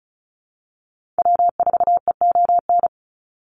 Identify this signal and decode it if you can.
Morse code, heard as a keyed tone, about 34 wpm, 700 Hz. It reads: W4EON
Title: NA SOTA Activator Callsigns 35wpm